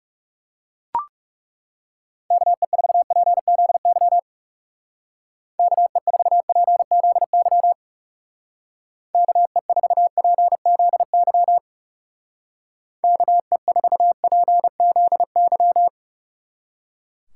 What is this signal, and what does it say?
Morse code, heard as a keyed tone, about 36 wpm, 700 Hz. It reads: KE4PZY KE4PZY KE4PZY KE4PZY
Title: Speed Racing - USA 2x3 Call Signs 30wpm